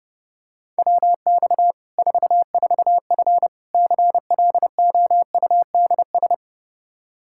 Morse code, heard as a keyed tone, about 30 wpm, 700 Hz. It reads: WX 44F CLOUDS
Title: K1USN SST QSO Practice 30wpm